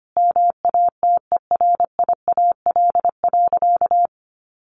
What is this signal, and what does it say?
Morse code, heard as a keyed tone, about 25 wpm, 700 Hz. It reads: MATERIAL.
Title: Sentences from Top 1100 Words 25wpm